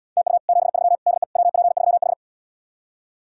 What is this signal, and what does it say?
Morse code, heard as a keyed tone, about 77 wpm, 700 Hz. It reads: TU 73 DE WP3H